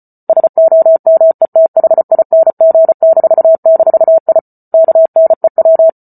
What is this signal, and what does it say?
Morse code, heard as a keyed tone, about 34 wpm, 650 Hz. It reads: SOMETHING--I KNEW